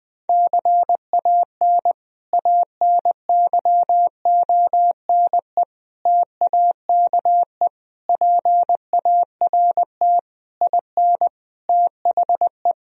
Morse code, heard as a keyed tone, about 20 wpm, 700 Hz. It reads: CAN ANYONE TAKE PART IN THE
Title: Sentences from Top 1000 Words - Review Part 4 - 20wpm